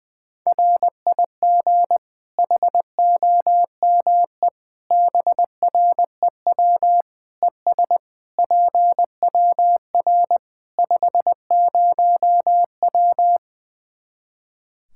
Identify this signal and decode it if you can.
Morse code, heard as a keyed tone, about 20 wpm, 700 Hz. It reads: RIG HOME BREW ES PWR 50W
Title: K1USN SST QSO Practice 20wpm